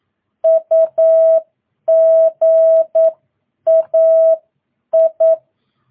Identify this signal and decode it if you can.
Morse code, heard as a keyed tone, about 9 wpm, spaced as for 8 wpm, 650 Hz. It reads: UGAI
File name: jh8cbh / 2022cw